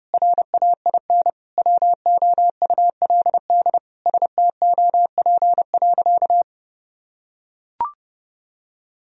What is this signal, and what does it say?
Morse code, heard as a keyed tone, about 32 wpm, 700 Hz. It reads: RAIN WOULD STOP.
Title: Sentences from Top 1000 Words - Review Part 1 - 30wpm